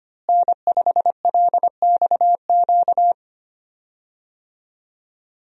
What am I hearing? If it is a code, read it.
Morse code, heard as a keyed tone, about 25 wpm, 700 Hz. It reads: N5LXQ
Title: USA Call Signs - 5 Characters - Repeated 2x - 25wpm